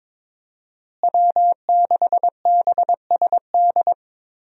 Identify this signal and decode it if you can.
Morse code, heard as a keyed tone, about 22 wpm, 700 Hz. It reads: W6BSD
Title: USA Call Signs - 5 Characters - Repeated 3x - 22wpm